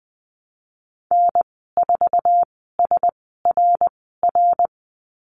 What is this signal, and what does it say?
Morse code, heard as a keyed tone, about 20 wpm, spaced as for 15 wpm, 700 Hz. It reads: N4SRR
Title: 076 Lesson 13g - Letter L - Call Signs 20wpm